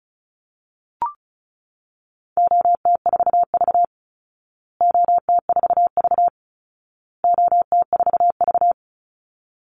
Morse code, heard as a keyed tone, about 34 wpm, 700 Hz. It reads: OT4V OT4V OT4V
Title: Call Signs Repeated 2x - 35wpm